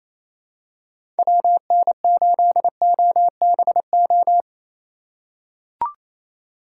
Morse code, heard as a keyed tone, about 28 wpm, 700 Hz. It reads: WN8OBO
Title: USA Call Signs - 6 Characters - 2x Character Spacing - 28wpm